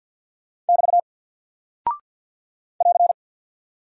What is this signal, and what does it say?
Morse code, heard as a keyed tone, about 48 wpm, 700 Hz. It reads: <BT> <AR>
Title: Single Character (Letter:Number:Punctuation:Prosign) Alt 50wpm